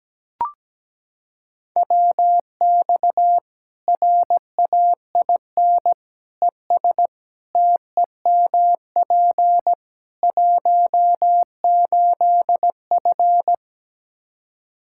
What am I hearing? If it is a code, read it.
Morse code, heard as a keyed tone, about 17 wpm, 700 Hz. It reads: WX RAIN ES TEMP 18F
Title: K1USN SST QSO Practice 17wpm